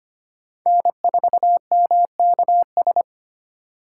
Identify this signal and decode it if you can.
Morse code, heard as a keyed tone, about 32 wpm, 700 Hz. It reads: N4MKS